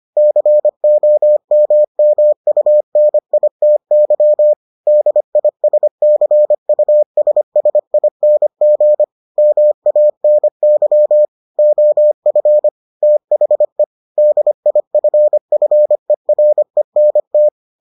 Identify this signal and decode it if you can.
Morse code, heard as a keyed tone, about 25 wpm, 600 Hz. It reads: COMMUNITY DISCUSSING MANY OF THE DIFFERENT